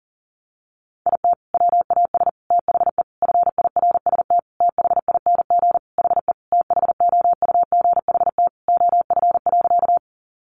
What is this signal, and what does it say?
Morse code, heard as a keyed tone, about 37 wpm, 700 Hz. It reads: IT WAS THE FIRST THING HE THOUGHT OF.
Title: Sentences from Top 300 Words 40wpm